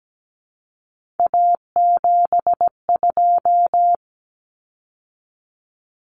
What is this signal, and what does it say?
Morse code, heard as a keyed tone, about 17 wpm, 700 Hz. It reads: A72
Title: Common Cut Numbers - 3-Digit 17wpm